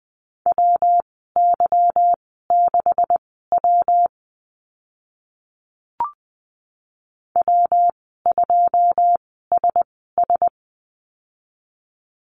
Morse code, heard as a keyed tone, about 20 wpm, spaced as for 15 wpm, 700 Hz. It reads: WY6W W2SS
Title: 154 Lesson 25g - Letter p - Call Signs 20wpm